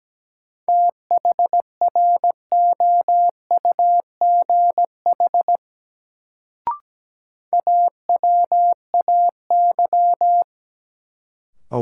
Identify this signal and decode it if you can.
Morse code, heard as a keyed tone, about 17 wpm, 700 Hz. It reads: THROUGH AWAY
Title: Sets of 2 Words - New Word Drill - 17wpm